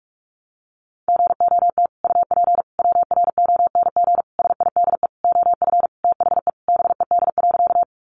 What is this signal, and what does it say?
Morse code, heard as a keyed tone, about 44 wpm, 700 Hz. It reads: GOT UP WRONG SIDE OF THE BED.
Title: Sentences from Top 900 Words 45wpm